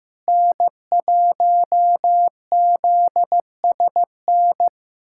Morse code, heard as a keyed tone, about 15 wpm, 700 Hz. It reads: N1ZSN
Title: USA Callsign DE Callsign - 5 Characters - 15wpm